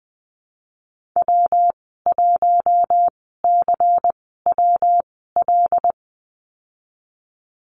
Morse code, heard as a keyed tone, about 20 wpm, spaced as for 15 wpm, 700 Hz. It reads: W1CWL
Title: 161 Lesson 26g - Letter g - Call Signs 20wpm